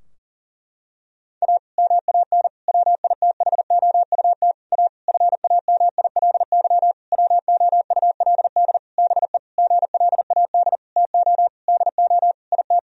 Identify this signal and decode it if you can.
Morse code, heard as a keyed tone, about 40 wpm, 700 Hz. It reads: A MAN WITHOUT A FAMILY WOULD BE GLAD TO DO IT
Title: Sentences from Top 1000 Words - Review Part 3 - 40wpm